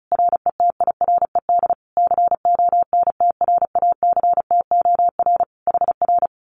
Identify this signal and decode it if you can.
Morse code, heard as a keyed tone, about 36 wpm, 700 Hz. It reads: RETIRED CONTRACTOR HR
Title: Common QSO Phrases Encore 35wpm